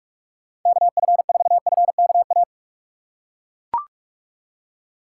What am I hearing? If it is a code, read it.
Morse code, heard as a keyed tone, about 42 wpm, 700 Hz. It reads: KF4FKA
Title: USA Call Signs - 6 Characters - 4x Character Spacing - 45wpm